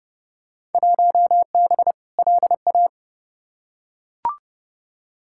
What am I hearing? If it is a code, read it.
Morse code, heard as a keyed tone, about 30 wpm, 700 Hz. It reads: 1B LA
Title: Field Day Exchanges Encore - 30wpm